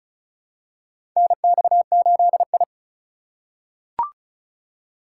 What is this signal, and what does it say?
Morse code, heard as a keyed tone, about 36 wpm, 700 Hz. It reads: NX8I
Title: USA Call Signs - 4 Characters - 3x Character Spacing - 35wpm